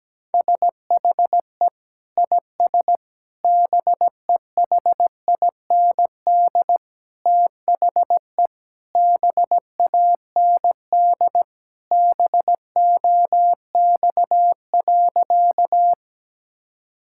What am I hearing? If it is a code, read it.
Morse code, heard as a keyed tone, about 17 wpm, 700 Hz. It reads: SHE IS BEHIND THE BAND BOX.
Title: Sentences from Top 1600 Words 17wpm